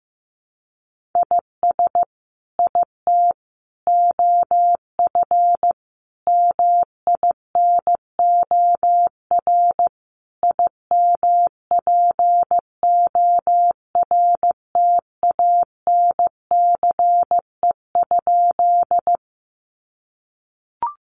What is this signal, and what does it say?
Morse code, heard as a keyed tone, about 15 wpm, 700 Hz. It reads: IS IT OF MINOR IMPORTANCE?
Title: Sentences from Top 1500 Words 15wpm